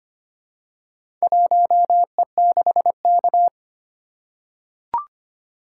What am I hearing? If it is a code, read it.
Morse code, heard as a keyed tone, about 25 wpm, 700 Hz. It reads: 1E6K
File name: practice / 25wpm / Four Letter-Number Combinations - 25wpm